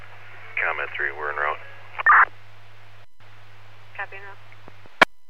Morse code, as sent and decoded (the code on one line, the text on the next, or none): none